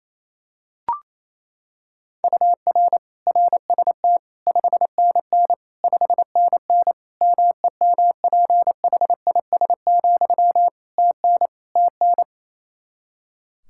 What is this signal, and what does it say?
Morse code, heard as a keyed tone, about 28 wpm, 700 Hz. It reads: UR RST 5NN 5NN MEMPHIS, TN TN
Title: Practice QSO - Exchange 1 - Beginning - 28wpm